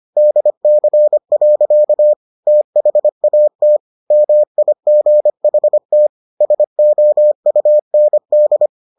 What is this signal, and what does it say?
Morse code, heard as a keyed tone, about 25 wpm, 600 Hz. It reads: DC. THAT MIGHT SOUND